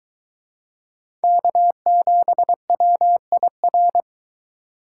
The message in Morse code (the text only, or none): K7WIR